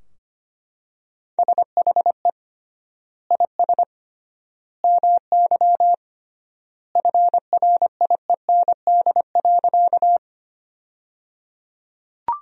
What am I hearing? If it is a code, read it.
Morse code, heard as a keyed tone, about 25 wpm, 700 Hz. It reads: SHE IS MY FRIEND.
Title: 3x Word-Spacing - Sentences from Top 300 Words - 25wpm